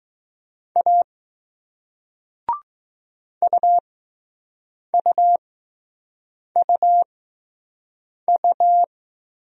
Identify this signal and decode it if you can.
Morse code, heard as a keyed tone, about 20 wpm, spaced as for 6 wpm, 700 Hz. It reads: A UUUU